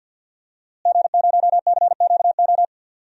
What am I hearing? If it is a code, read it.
Morse code, heard as a keyed tone, about 51 wpm, 700 Hz. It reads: G0CQO
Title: DXCC-100 Call Signs - 5 Characters - Repeated 2x - 50wpm